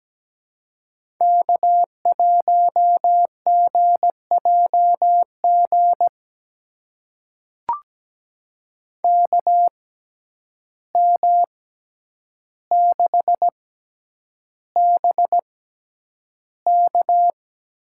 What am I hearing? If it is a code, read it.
Morse code, heard as a keyed tone, about 17 wpm, spaced as for 9 wpm, 700 Hz. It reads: K1GJG KM6BK
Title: USA Call Signs - 5 Characters - 6x Character Spacing - 17wpm